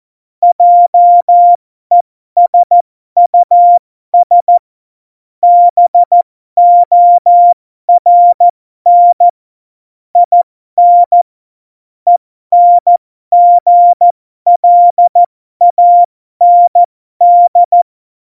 Morse code, heard as a keyed tone, about 14 wpm, spaced as for 12 wpm, 700 Hz. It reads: JESUS BORN IN ENGLAND
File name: Morsekurs / Witze